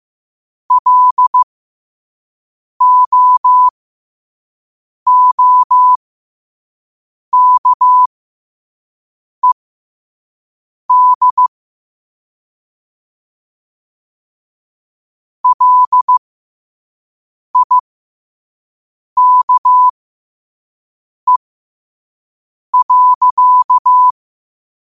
Morse code, heard as a keyed tone, about 15 wpm, spaced as for 5 wpm, 1000 Hz. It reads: LOOKED LIKE.
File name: qotdfiles / 5wpm